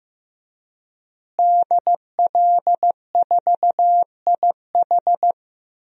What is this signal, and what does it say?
Morse code, heard as a keyed tone, about 15 wpm, 700 Hz. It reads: DL4IH
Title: DXCC-100 Call Signs - 5 Characters - Repeated 3x - 15wpm